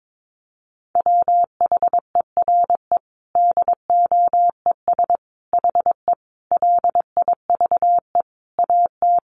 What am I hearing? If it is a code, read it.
Morse code, heard as a keyed tone, about 22 wpm, 700 Hz. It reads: WHERE DOES HE LIVE AT
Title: Sentences from Top 800 Words 22wpm